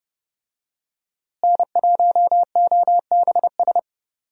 Morse code, heard as a keyed tone, about 30 wpm, 700 Hz. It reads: N1OBS
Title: USA Call Signs - 5 Characters - Repeated 3x - 30wpm